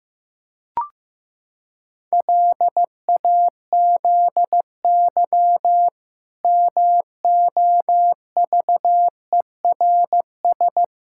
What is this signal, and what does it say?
Morse code, heard as a keyed tone, about 15 wpm, 700 Hz. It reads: LAZY MOVERS
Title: Pangram Challenge v2 - 15wpm